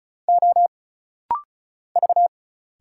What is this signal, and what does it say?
Morse code, heard as a keyed tone, about 36 wpm, 700 Hz. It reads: O V
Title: Single Letters - Rapid-Fire 35wpm